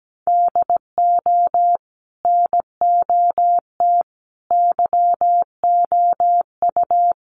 Morse code, heard as a keyed tone, about 17 wpm, 700 Hz. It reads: DO NOT YOU
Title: Sets of 4 Words 17wpm